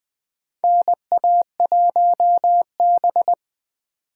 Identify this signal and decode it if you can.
Morse code, heard as a keyed tone, about 20 wpm, 700 Hz. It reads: NA1B
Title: USA Call Signs - 4 Characters - 3x Character Spacing - 20wpm